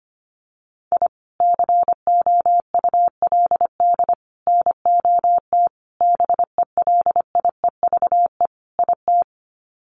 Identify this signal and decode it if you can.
Morse code, heard as a keyed tone, about 25 wpm, 700 Hz. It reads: I COULD NOT BELIEVE IT
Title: Sets of 5 Words Encore 25wpm